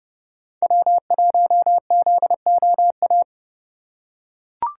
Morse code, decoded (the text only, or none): W1ZOA